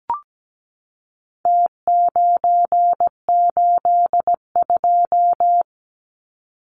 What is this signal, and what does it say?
Morse code, heard as a keyed tone, about 17 wpm, 700 Hz. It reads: T982
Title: Numbers - Contest Style - 4-Digit 17wpm